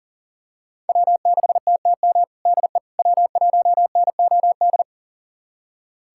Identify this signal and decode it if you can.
Morse code, heard as a keyed tone, about 40 wpm, 700 Hz. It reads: W6TTM DE W1NOD
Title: USA Callsign DE Callsign - 5 Characters - 40wpm